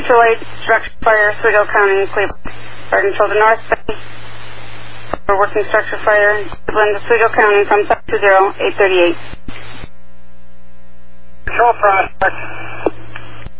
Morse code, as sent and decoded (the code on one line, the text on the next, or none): none